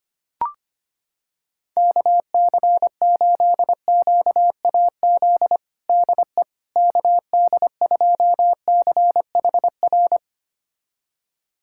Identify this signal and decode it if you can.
Morse code, heard as a keyed tone, about 25 wpm, 700 Hz. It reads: KC8QAZ DE KD2CHR
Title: USA Callsign DE Callsign - 6 Characters - 25wpm